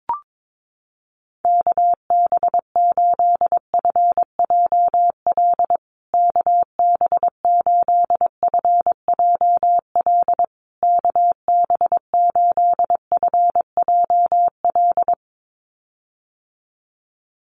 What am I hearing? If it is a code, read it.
Morse code, heard as a keyed tone, about 22 wpm, 700 Hz. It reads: KB8FJL KB8FJL KB8FJL
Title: USA Call Signs Repeated 3x 22wpm